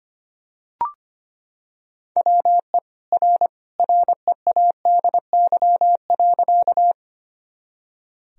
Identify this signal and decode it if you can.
Morse code, heard as a keyed tone, about 25 wpm, 700 Hz. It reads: WE R READY.
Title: Sentences from Top 400 Words and Abbreviations 25wpm